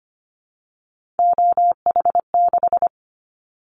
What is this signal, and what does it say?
Morse code, heard as a keyed tone, about 25 wpm, 700 Hz. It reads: OH6
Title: Callsign Prefixes for Top 50 DXCC - 25wpm